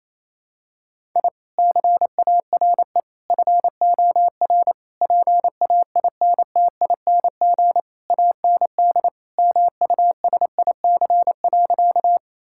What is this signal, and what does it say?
Morse code, heard as a keyed tone, about 28 wpm, 700 Hz. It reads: I CARE FOR PAINTING AND MUSIC.
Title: Sentences from Top 1000 Words - Review Part 6 - 28wpm